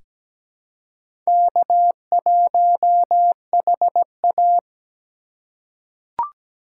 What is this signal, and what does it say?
Morse code, heard as a keyed tone, about 17 wpm, 700 Hz. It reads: K1HA
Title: USA Call Signs - 4 Characters - 4x Character Spacing - 17wpm